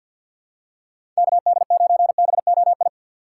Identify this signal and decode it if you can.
Morse code, heard as a keyed tone, about 50 wpm, 700 Hz. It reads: KD9BOI
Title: USA Call Signs - 6 Characters - 3x Character Spacing - 50wpm